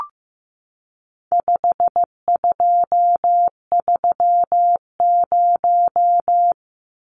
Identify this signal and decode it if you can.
Morse code, heard as a keyed tone, about 15 wpm, 700 Hz. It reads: 5230
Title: Numbers - 4-Digit 15wpm